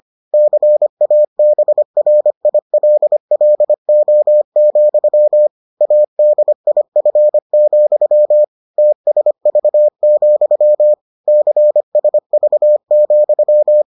Morse code, heard as a keyed tone, about 25 wpm, 600 Hz. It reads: CABRILLO, ADIF, TSV, CSV,